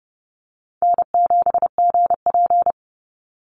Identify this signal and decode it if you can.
Morse code, heard as a keyed tone, about 30 wpm, 700 Hz. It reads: N7GP
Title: USA Call Signs 30wpm